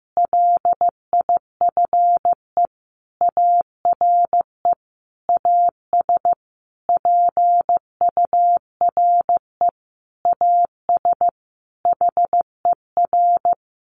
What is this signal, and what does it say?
Morse code, heard as a keyed tone, about 15 wpm, 700 Hz. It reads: LIFE ARE AS PURE AS HER
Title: Sentences from Top 1500 Words 15wpm